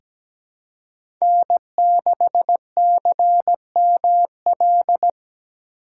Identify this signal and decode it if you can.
Morse code, heard as a keyed tone, about 17 wpm, 700 Hz. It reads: N6CML
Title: USA Call Signs - 4x Character Spacing - 17wpm